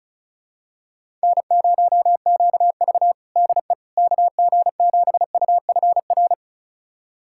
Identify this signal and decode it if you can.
Morse code, heard as a keyed tone, about 36 wpm, 700 Hz. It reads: N0QV DE KG7UFR
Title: USA Callsign DE Callsign 35wpm